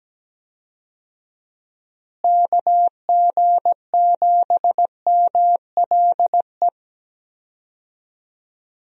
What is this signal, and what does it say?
Morse code, heard as a keyed tone, about 17 wpm, 700 Hz. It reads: KG7MLE
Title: USA Call Signs - 6 Characters - Repeated 2x - 17wpm